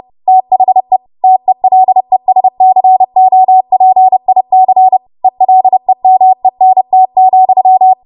Morse code, heard as a keyed tone, about 30 wpm, 750 Hz. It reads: THE TELESCOPIC ELEMENT,
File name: Morse / 30WPM